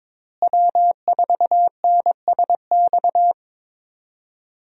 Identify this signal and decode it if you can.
Morse code, heard as a keyed tone, about 21 wpm, 700 Hz. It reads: W4NSX
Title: USA Call Signs - 5 Characters - Repeated 3x - 22wpm